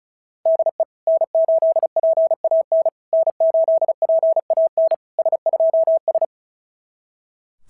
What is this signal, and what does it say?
Morse code, heard as a keyed tone, about 36 wpm, 690 Hz. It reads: DE N8PAN N8PAN S2S
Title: SOTA QSO Practice 35wpm